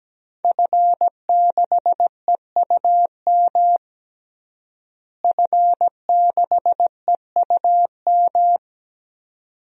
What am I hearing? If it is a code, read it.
Morse code, heard as a keyed tone, about 17 wpm, 700 Hz. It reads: F6EUM F6EUM
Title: DXCC-100 Call Signs - 5 Characters - Repeated 2x - 17wpm